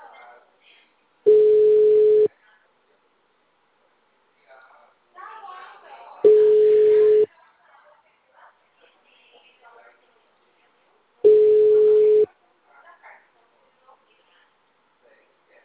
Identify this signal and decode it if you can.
Ringback tone (Europe, 425 Hz), with speech.